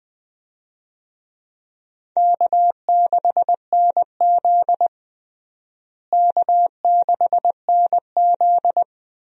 Morse code, heard as a keyed tone, about 20 wpm, 700 Hz. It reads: K6NZ K6NZ
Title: USA Call Signs - 4 Characters - Repeated 3x - 20wpm